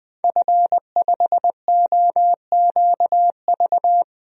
Morse code, heard as a keyed tone, about 20 wpm, 700 Hz. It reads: F5OQV
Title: DXCC-100 Call Signs - 5 Characters - 20wpm